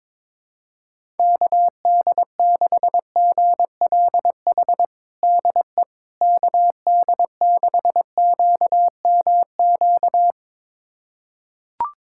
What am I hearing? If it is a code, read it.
Morse code, heard as a keyed tone, about 22 wpm, 700 Hz. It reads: KD6GLH DE KD6QMQ
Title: USA Callsign DE Callsign - 6 Characters - 22wpm